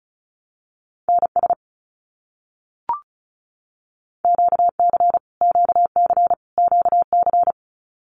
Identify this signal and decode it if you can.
Morse code, heard as a keyed tone, about 34 wpm, 700 Hz. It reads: NS QC QC QC
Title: Canadian Provinces and Territories - Repeated 3x - 35wpm